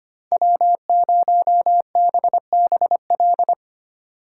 Morse code, heard as a keyed tone, about 29 wpm, 700 Hz. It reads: W0BBL
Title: Speed Racing - USA 1x3 Call Signs 25wpm